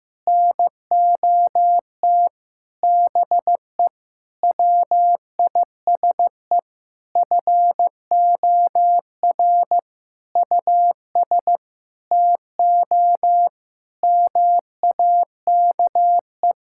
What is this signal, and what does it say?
Morse code, heard as a keyed tone, about 15 wpm, 700 Hz. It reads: NOT BE WISE FOR US TO MAKE
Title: Sentences from Top 1600 Words 15wpm